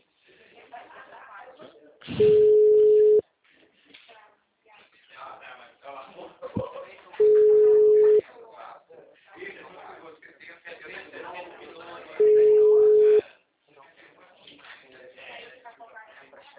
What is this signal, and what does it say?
Ringback tone (Europe, 425 Hz), with speech.
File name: out- -2016- - - .19384